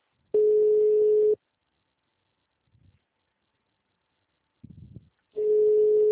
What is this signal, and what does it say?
Ringback tone (Europe, 425 Hz).